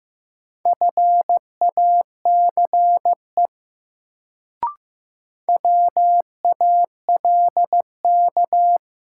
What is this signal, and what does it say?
Morse code, heard as a keyed tone, about 15 wpm, 700 Hz. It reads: FACE WALK
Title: Sentences from Top 300 Words - New Word Drill - 15wpm